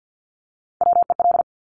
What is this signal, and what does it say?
Morse code, heard as a keyed tone, about 50 wpm, 700 Hz. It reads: WEL
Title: Top Three Letter Combinations in Words 50wpm